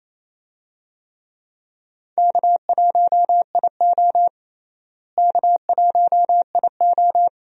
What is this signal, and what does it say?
Morse code, heard as a keyed tone, about 27 wpm, 700 Hz. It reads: K1IO K1IO
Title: USA Call Signs - 4 Characters - Repeated 2x - 28wpm